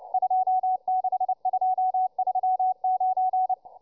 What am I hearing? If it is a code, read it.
Morse code, heard as a keyed tone, about 29 wpm, 740 Hz, with radio noise. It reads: 26239